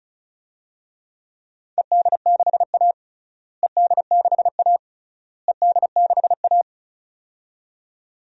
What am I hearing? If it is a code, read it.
Morse code, heard as a keyed tone, about 36 wpm, 700 Hz. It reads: ED6A ED6A ED6A
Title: DXCC-100 Call Signs - 4 Characters - Repeated 3x - 35wpm